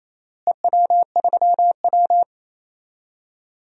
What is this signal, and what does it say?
Morse code, heard as a keyed tone, about 27 wpm, 700 Hz. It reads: EW3W
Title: DXCC-100 Call Signs - 4 Characters - Repeated 2x - 28wpm